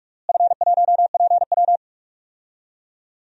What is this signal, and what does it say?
Morse code, heard as a keyed tone, about 43 wpm, 700 Hz. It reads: F1PW
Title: DXCC-100 Call Signs - 4 Characters - Repeated 2x - 45wpm